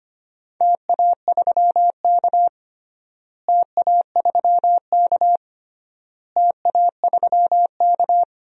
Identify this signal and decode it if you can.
Morse code, heard as a keyed tone, about 25 wpm, 700 Hz. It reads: TA3K TA3K TA3K
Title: DXCC-100 Call Signs - 4 Characters - Repeated 3x - 25wpm